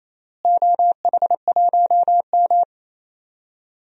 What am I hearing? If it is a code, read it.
Morse code, heard as a keyed tone, about 28 wpm, 700 Hz. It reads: OH1M
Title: DXCC-100 Call Signs - 4 Characters - Repeated 3x - 28wpm